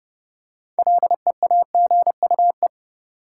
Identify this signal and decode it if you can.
Morse code, heard as a keyed tone, about 30 wpm, 700 Hz. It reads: LEAGUE
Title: Six Letter Words - 30wpm